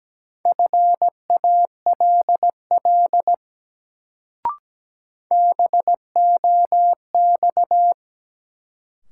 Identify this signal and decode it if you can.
Morse code, heard as a keyed tone, about 17 wpm, 700 Hz. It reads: FALL BOX
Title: Sentences from Top 500 Words - New Word Drill - 17wpm